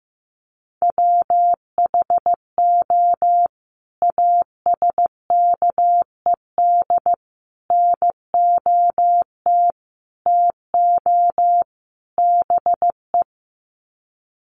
Morse code, heard as a keyed tone, about 15 wpm, 700 Hz. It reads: WHO ASKED NOT TO BE
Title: Sets of 5 Words Encore2 15wpm